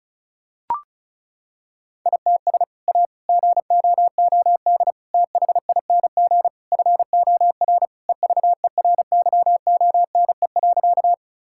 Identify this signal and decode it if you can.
Morse code, heard as a keyed tone, about 34 wpm, 700 Hz. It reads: ITS A GOOD THING FOR EVERYONE.
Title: Sentences from Top 1000 Words - Review Part 5 - 35wpm